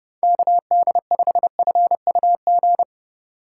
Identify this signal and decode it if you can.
Morse code, heard as a keyed tone, about 31 wpm, 700 Hz. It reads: KD5FUG